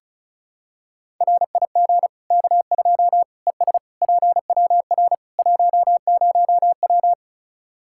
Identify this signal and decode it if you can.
Morse code, heard as a keyed tone, about 36 wpm, 700 Hz. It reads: RIG K2 ES PWR 10W
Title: Practice QSO - Exchange 2 - Info - 35wpm